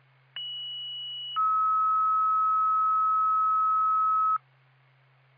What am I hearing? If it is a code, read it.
Two-tone page, heard as a one-second tone, then a three-second tone, about 2950 then 1290 Hz. It is heard from 0.4 to 4.4 s.